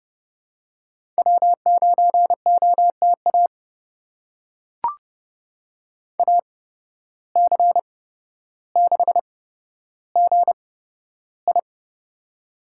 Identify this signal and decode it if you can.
Morse code, heard as a keyed tone, about 30 wpm, spaced as for 13 wpm, 700 Hz. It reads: W9OTA AC6GI
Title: USA Call Signs - 5 Characters - 8x Character Spacing - 30wpm